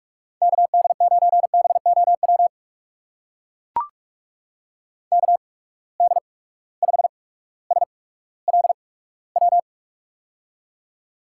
Morse code, heard as a keyed tone, about 44 wpm, spaced as for 19 wpm, 700 Hz. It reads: KD9BOW KD5SLW
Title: USA Call Signs - 6 Characters - 8x Character Spacing - 45wpm